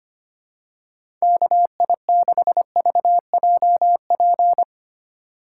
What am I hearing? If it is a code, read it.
Morse code, heard as a keyed tone, about 25 wpm, 700 Hz. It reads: KI6VJP
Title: USA Call Signs - 6 Characters - 2x Character Spacing - 25wpm